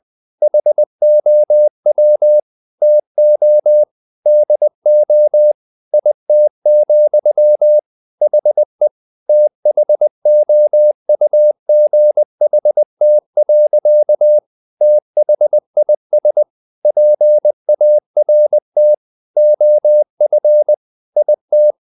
Morse code, heard as a keyed tone, about 20 wpm, 600 Hz. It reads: HOW TO DO IT, HE THOUGHT. THIS PART OF IT
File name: the old man and the sea part 2 20wpm